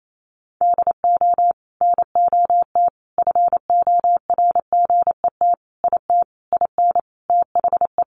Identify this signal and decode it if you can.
Morse code, heard as a keyed tone, about 28 wpm, 700 Hz. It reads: DO NOT FORGET IT IN THE
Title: Sentences from Top 1200 Words 28wpm